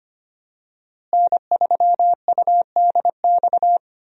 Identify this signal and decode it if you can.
Morse code, heard as a keyed tone, about 25 wpm, 700 Hz. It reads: N3UDX
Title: USA Call Signs - 5 Characters - Repeated 3x - 25wpm